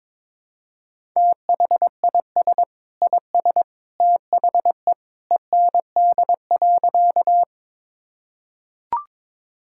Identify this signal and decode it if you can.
Morse code, heard as a keyed tone, about 22 wpm, 700 Hz. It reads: THIS IS THE END.
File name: practice / 22wpm / Sentences with 2-Word Phrases - 22wpm